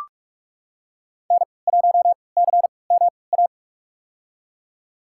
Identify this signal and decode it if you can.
Morse code, heard as a keyed tone, about 46 wpm, spaced as for 26 wpm, 700 Hz. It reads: N1CMA
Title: USA Call Signs - 3x Character Spacing - 45wpm